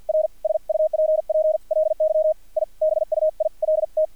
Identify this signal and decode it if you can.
Morse code, heard as a keyed tone, about 50 wpm, spaced as for 42 wpm, 630 Hz. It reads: R SUPPLY I BUILT
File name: archived / CRSnet